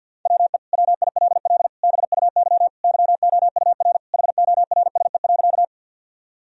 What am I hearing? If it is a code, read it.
Morse code, heard as a keyed tone, about 51 wpm, 700 Hz. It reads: WE WILL BUY YOUR HORSE.
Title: Sentences from Top 1000 Words - Review Part 4 - 50wpm